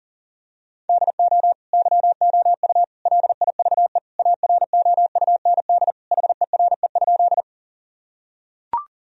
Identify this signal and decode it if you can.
Morse code, heard as a keyed tone, about 40 wpm, 700 Hz. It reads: DO YOU LIVE AROUND HERE?
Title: Sentences from Top 1000 Words - Review Part 3 - 40wpm